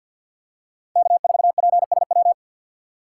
Morse code, heard as a keyed tone, about 52 wpm, 700 Hz. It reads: K4PSW
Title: USA Call Signs - 5 Characters - 3x Character Spacing - 50wpm